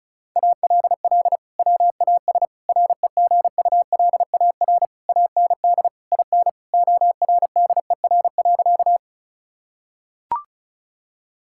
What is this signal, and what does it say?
Morse code, heard as a keyed tone, about 34 wpm, 700 Hz. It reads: ALL WAS REGULAR AND IN ORDER.
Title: Sentences from Top 1000 Words - Review Part 6 - 35wpm